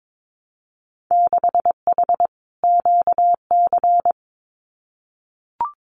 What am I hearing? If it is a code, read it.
Morse code, heard as a keyed tone, about 22 wpm, 700 Hz. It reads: 6H QC
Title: Winter Field Day Exchanges Encore 22wpm